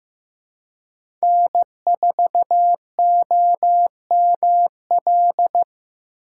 Morse code, heard as a keyed tone, about 15 wpm, 700 Hz. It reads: N4OML
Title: USA Call Signs - 5 Characters - 4x Character Spacing - 15wpm